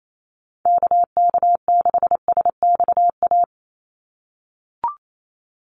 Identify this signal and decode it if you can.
Morse code, heard as a keyed tone, about 28 wpm, 700 Hz. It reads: KK6SXA
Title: USA Call Signs 28wpm